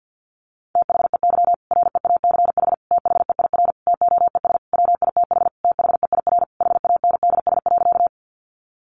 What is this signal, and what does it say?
Morse code, heard as a keyed tone, about 50 wpm, 700 Hz. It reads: THEY REACH THEIR TOES WITH THEIR HANDS.
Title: Easy Phrases 50wpm